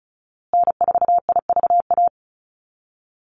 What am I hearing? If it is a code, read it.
Morse code, heard as a keyed tone, about 34 wpm, 700 Hz. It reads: N4IVA
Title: USA Call Signs Repeated 2x 35wpm